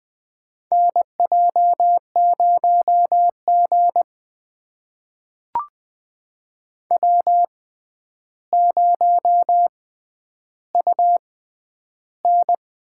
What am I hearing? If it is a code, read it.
Morse code, heard as a keyed tone, about 21 wpm, spaced as for 10 wpm, 700 Hz. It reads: NJ0G W0UN
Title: USA Call Signs - 4 Characters - 6x Character Spacing - 20wpm